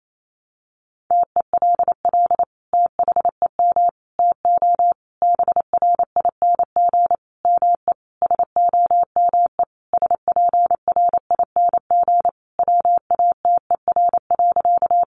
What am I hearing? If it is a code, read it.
Morse code, heard as a keyed tone, about 28 wpm, 700 Hz. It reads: TELL THEM TO BRING ME SOME SPRING WATER.
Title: Sentences from Top 1200 Words 28wpm